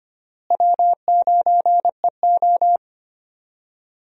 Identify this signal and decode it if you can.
Morse code, heard as a keyed tone, about 25 wpm, 700 Hz. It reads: W9EO
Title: USA Call Signs - 4 Characters - Repeated 3x - 25wpm